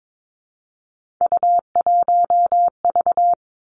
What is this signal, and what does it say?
Morse code, heard as a keyed tone, about 22 wpm, 700 Hz. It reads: U1V
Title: Three Challenging Letter-Number Combinations 22wpm